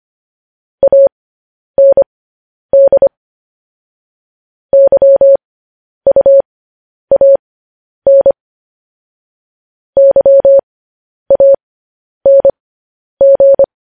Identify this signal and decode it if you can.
Morse code, heard as a keyed tone, about 25 wpm, spaced as for 10 wpm, 550 Hz. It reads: AND YUAN YANG
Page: news headlines in morse code at 10 wpm